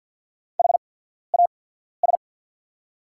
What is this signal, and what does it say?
Morse code, heard as a keyed tone, about 48 wpm, spaced as for 14 wpm, 700 Hz. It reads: HAS
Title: 8x Character Spacing - Top 100 Words - 50wpm